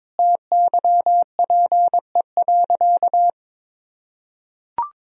Morse code, heard as a keyed tone, about 22 wpm, 700 Hz. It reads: TYPE.
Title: Sentences from Top 1000 Words - Review Part 3 - 22wpm